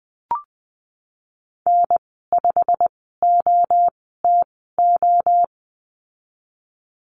Morse code, heard as a keyed tone, about 20 wpm, spaced as for 15 wpm, 700 Hz. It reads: N5OTO
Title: 154 Lesson 25g - Letter p - Call Signs 20wpm